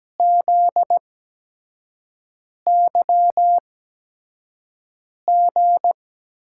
Morse code, heard as a keyed tone, about 17 wpm, spaced as for 5 wpm, 700 Hz. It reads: ZYG
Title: USA Call Signs - 5 Characters - 8x Character Spacing - 17wpm